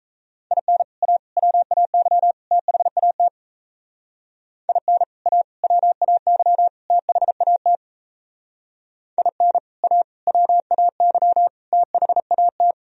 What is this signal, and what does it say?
Morse code, heard as a keyed tone, about 34 wpm, 700 Hz. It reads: IN A WAY THAT IN A WAY THAT IN A WAY THAT
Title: Speed-Racing - Sets of 4 Words 28wpm